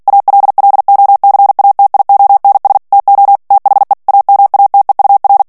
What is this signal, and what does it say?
Morse code, heard as a keyed tone, about 48 wpm, 800 Hz. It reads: ALLOCATIONS TO THE AMATEUR
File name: ANGLAIS texte clair 3 vitesse 45 mots minute